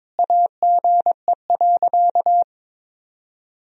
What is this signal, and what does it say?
Morse code, heard as a keyed tone, about 22 wpm, 700 Hz. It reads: AGE.
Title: Sentences from Top 1000 Words - Review Part 3 - 22wpm